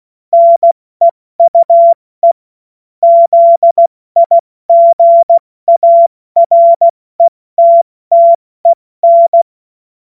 Morse code, heard as a keyed tone, about 16 wpm, 680 Hz. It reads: NEUE ZIGARETTEN
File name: Morsekurs / Witze / Witz 03. August 2021, 70 F80